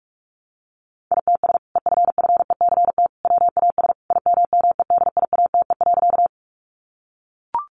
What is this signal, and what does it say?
Morse code, heard as a keyed tone, about 44 wpm, 700 Hz. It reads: ITS EFFECT WAS IMMEDIATE.
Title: Sentences from Top 1200 Words 45wpm